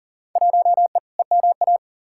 Morse code, heard as a keyed tone, about 40 wpm, 700 Hz. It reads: 1E EMA
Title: Field Day Exchanges Encore - 40wpm